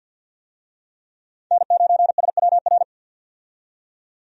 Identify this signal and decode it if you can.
Morse code, heard as a keyed tone, about 50 wpm, 700 Hz. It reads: N9SWR
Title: USA Call Signs - 5 Characters - 50wpm